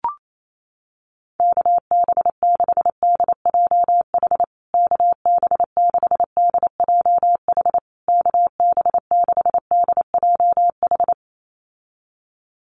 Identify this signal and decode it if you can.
Morse code, heard as a keyed tone, about 28 wpm, 700 Hz. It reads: KB6DJH KB6DJH KB6DJH
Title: USA Call Signs Repeated 3x 28wpm